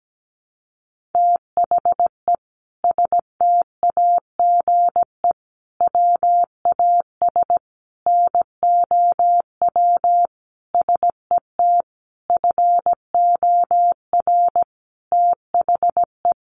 Morse code, heard as a keyed tone, about 17 wpm, 700 Hz. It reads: THE STAGE WAS NOW SET FOR THE
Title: Sentences from Top 1500 Words 17wpm